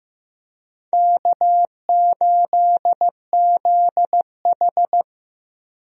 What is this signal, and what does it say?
Morse code, heard as a keyed tone, about 15 wpm, 700 Hz. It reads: K8ZH
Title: USA Call Signs - 4 Characters - Repeated 3x - 15wpm